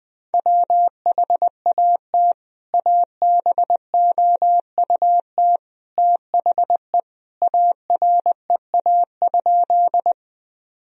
Morse code, heard as a keyed tone, about 20 wpm, 700 Hz. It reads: WHAT ABOUT THE AREA?
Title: Sentences from Top 1000 Words - Review Part 2 - 20wpm